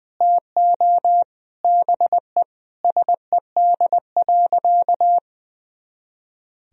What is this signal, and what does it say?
Morse code, heard as a keyed tone, about 19 wpm, 700 Hz. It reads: TO BE SED.
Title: Sentences from Top 100 Words and Abbreviations 20wpm